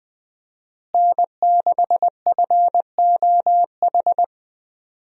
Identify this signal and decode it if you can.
Morse code, heard as a keyed tone, about 20 wpm, 700 Hz. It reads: N6FOH
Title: USA Call Signs - 5 Characters - Repeated 3x - 20wpm